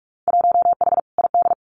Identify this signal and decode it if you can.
Morse code, heard as a keyed tone, about 44 wpm, 700 Hz. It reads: 1H ID
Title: Winter Field Day Exchanges 45wpm